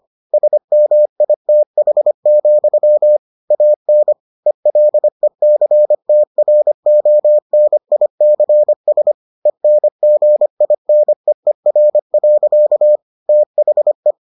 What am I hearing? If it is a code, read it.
Morse code, heard as a keyed tone, about 25 wpm, 600 Hz. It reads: SMITH, AN ELECTRONICS ENGINEER. THE